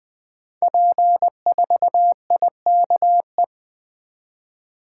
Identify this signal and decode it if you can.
Morse code, heard as a keyed tone, about 20 wpm, 700 Hz. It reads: P4IKE
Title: Code Groups - 5 Letter:Numbers - 20wpm